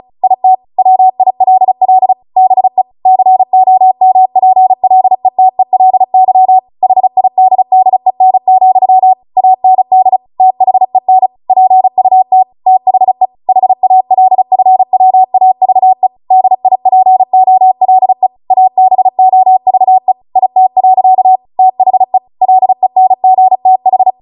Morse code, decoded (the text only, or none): IT WILL BE COMPLETELY HIDDEN, AND THEN PUT THE HALFWAVE DIPOLE ABOVE IT. THE LENGTH